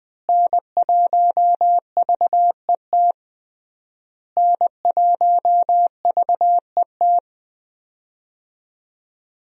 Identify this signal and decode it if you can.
Morse code, heard as a keyed tone, about 20 wpm, 700 Hz. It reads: N1VET N1VET
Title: USA Call Signs - 5 Characters - Repeated 3x - 20wpm